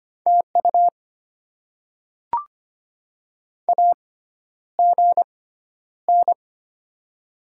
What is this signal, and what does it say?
Morse code, heard as a keyed tone, about 25 wpm, spaced as for 11 wpm, 700 Hz. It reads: TU AGN
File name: Top 20 QSO Elements - 6x Character Spacing - 25wpm